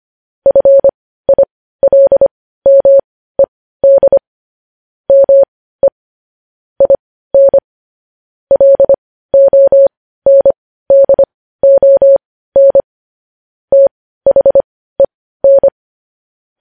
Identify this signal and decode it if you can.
Morse code, heard as a keyed tone, about 25 wpm, spaced as for 15 wpm, 550 Hz. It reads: FILMED ME IN LONDON THEN
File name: podcast / News headlines-1x-25x15